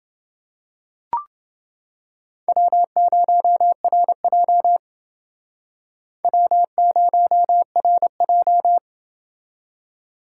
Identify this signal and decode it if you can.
Morse code, heard as a keyed tone, about 28 wpm, 700 Hz. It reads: W0RJ W0RJ